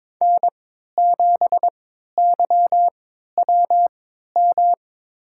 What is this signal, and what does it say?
Morse code, heard as a keyed tone, about 22 wpm, spaced as for 13 wpm, 700 Hz. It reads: N7YWM
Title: USA Call Signs - 5 Characters - 3x Character Spacing - 22wpm